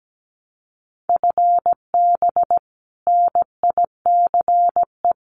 Morse code, heard as a keyed tone, about 17 wpm, 700 Hz. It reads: FB NICE
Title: Common QSO Phrases 17wpm